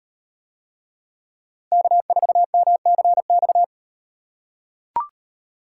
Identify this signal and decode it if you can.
Morse code, heard as a keyed tone, about 38 wpm, 700 Hz. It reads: K4MCX